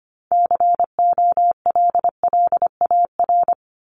Morse code, heard as a keyed tone, about 25 wpm, 700 Hz. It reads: COLLAR
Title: Top 1751-2000 Words 25wpm